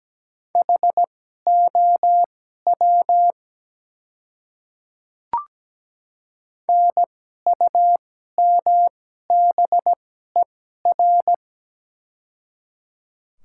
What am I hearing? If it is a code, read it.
Morse code, heard as a keyed tone, about 17 wpm, spaced as for 12 wpm, 700 Hz. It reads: HOW NUMBER
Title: 2x Character Spacing - Top 100 Words - 17wpm